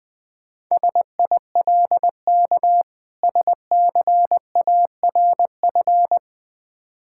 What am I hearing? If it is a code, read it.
Morse code, heard as a keyed tone, about 21 wpm, 700 Hz. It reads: SILK SCARF
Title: Difficult Words with Hints - 20wpm